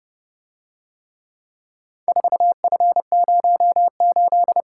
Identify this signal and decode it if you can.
Morse code, heard as a keyed tone, about 29 wpm, 700 Hz. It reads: 4F08